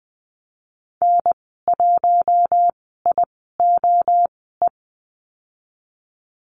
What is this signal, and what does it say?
Morse code, heard as a keyed tone, about 20 wpm, spaced as for 14 wpm, 700 Hz. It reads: N1IOE
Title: 042 Lesson 8g - Number 1 - Call Signs 20wpm